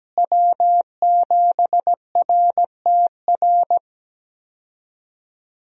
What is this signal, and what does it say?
Morse code, heard as a keyed tone, about 17 wpm, 700 Hz. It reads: W7RTR
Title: USA Call Signs - 5 Characters - Repeated 3x - 17wpm